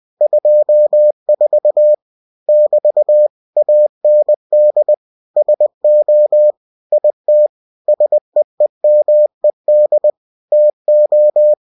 Morse code, heard as a keyed tone, about 20 wpm, 600 Hz. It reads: 24 <BT> AND SO IT SEEMED TO
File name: cw-books / winnie-the-pooh / winnie-20wpm-page-0024